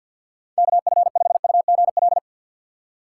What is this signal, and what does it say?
Morse code, heard as a keyed tone, about 47 wpm, 700 Hz. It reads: KF5UGL
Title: USA Call Signs - 4x Character Spacing - 50wpm